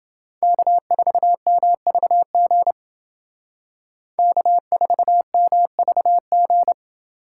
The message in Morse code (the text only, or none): K4MVG K4MVG